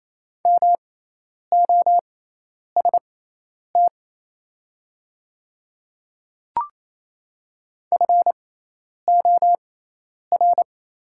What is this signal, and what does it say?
Morse code, heard as a keyed tone, about 27 wpm, spaced as for 10 wpm, 700 Hz. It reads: MOST FOR
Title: 6x Character Spacing - Top 300 Words - 28wpm